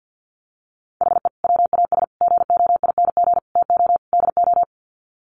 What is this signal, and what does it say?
Morse code, heard as a keyed tone, about 50 wpm, 700 Hz. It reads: HE WAS GOING TO DO
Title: Sets of 5 Words Encore2 50wpm